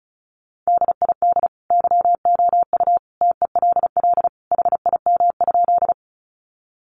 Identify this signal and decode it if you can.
Morse code, heard as a keyed tone, about 34 wpm, 700 Hz. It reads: DID YOU TELL HIM?
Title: Sentences from Top 200 Words 35wpm